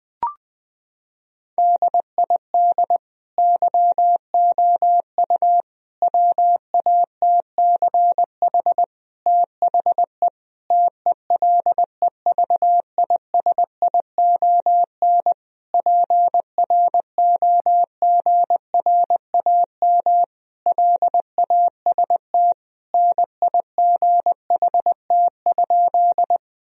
Morse code, heard as a keyed tone, about 20 wpm, 700 Hz. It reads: DID YOU WATCH THE TELEVISION PROGRAM LAST NIGHT?
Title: Sentences from Top 1000 Words - Review Part 4 - 20wpm